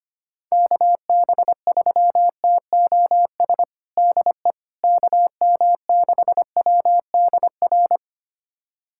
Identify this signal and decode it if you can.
Morse code, heard as a keyed tone, about 25 wpm, 700 Hz. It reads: KB3TOS DE KM6WDR
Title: USA Callsign DE Callsign - 6 Characters - 25wpm